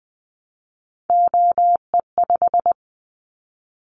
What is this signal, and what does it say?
Morse code, heard as a keyed tone, about 20 wpm, 700 Hz. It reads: OE5
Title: Callsign Prefixes for Top 50 DXCC - 20wpm